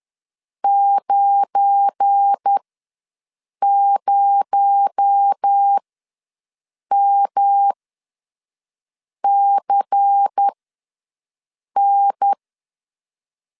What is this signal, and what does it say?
Morse code, heard as a keyed tone, about 10 wpm, spaced as for 5 wpm, 800 Hz. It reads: 90MCN
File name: LEZIONI / 8 YJ45 / 8 YJ4 ws 10wpm 4020